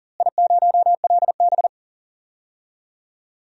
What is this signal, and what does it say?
Morse code, heard as a keyed tone, about 40 wpm, 700 Hz. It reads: I0LB
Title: DXCC-100 Call Signs - 4 Characters - Repeated 3x - 40wpm